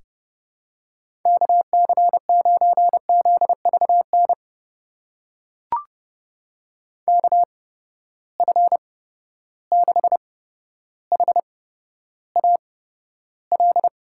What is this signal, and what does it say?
Morse code, heard as a keyed tone, about 30 wpm, spaced as for 13 wpm, 700 Hz. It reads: KC9ZVN KF6HAL
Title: USA Call Signs - 6 Characters - 8x Character Spacing - 30wpm